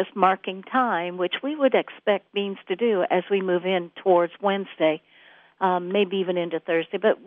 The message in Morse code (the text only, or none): none